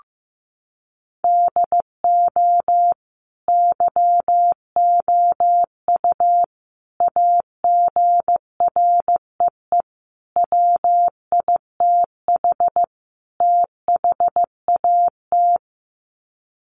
Morse code, heard as a keyed tone, about 15 wpm, 700 Hz. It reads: DO YOU AGREE WITH THAT
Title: Sets of 5 Words Encore 15wpm